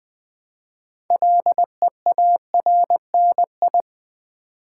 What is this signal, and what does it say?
Morse code, heard as a keyed tone, about 20 wpm, 700 Hz. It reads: LEARNI
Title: Build A Word - 2 Letters At A Time - 8-Letter Words - 20wpm